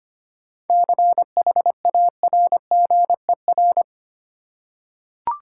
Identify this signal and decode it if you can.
Morse code, heard as a keyed tone, about 25 wpm, 690 Hz. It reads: CHARGER
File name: practice / 25wpm / Easy Words with Hints - 25wpm